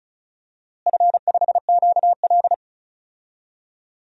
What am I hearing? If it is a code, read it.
Morse code, heard as a keyed tone, about 36 wpm, 700 Hz. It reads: F5QL
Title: DXCC-100 Call Signs - 4 Characters - Repeated 3x - 35wpm